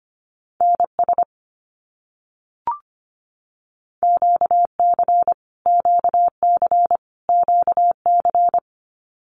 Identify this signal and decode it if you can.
Morse code, heard as a keyed tone, about 25 wpm, 700 Hz. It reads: NS QC QC QC
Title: Canadian Provinces and Territories - Repeated 3x - 25wpm